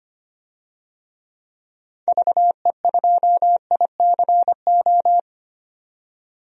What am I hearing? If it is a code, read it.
Morse code, heard as a keyed tone, about 25 wpm, 700 Hz. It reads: VE2ICO
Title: DXCC-100 Call Signs - 6 Characters - 25wpm